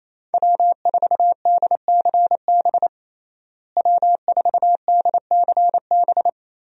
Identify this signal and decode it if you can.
Morse code, heard as a keyed tone, about 27 wpm, 700 Hz. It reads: W4DCB W4DCB
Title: USA Call Signs - 5 Characters - Repeated 3x - 28wpm